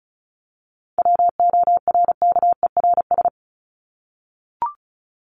Morse code, heard as a keyed tone, about 34 wpm, 700 Hz. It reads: WORKERS
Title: ES:S Suffix - Top 100 Words with Base Words 35wpm